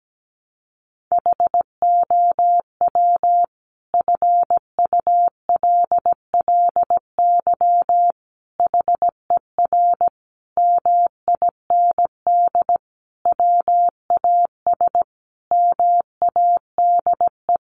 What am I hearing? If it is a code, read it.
Morse code, heard as a keyed tone, about 17 wpm, 700 Hz. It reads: HOW FULLY HER MIND WAS MADE
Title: Sentences from Top 1300 Words 17wpm